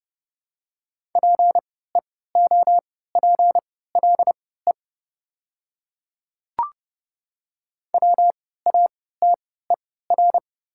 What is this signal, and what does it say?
Morse code, heard as a keyed tone, about 30 wpm, spaced as for 17 wpm, 700 Hz. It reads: PEOPLE WATER
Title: 3x Character Spacing - Top 100 Words - 30wpm